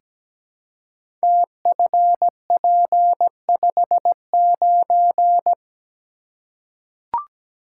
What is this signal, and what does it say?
Morse code, heard as a keyed tone, about 17 wpm, 700 Hz. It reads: TFP59
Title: Code Groups - 5 Letter:Numbers - 17wpm